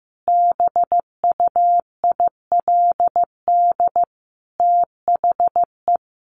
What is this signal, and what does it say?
Morse code, heard as a keyed tone, about 14 wpm, 700 Hz. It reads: BUILD THE
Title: Sentences from Top 1500 Words 15wpm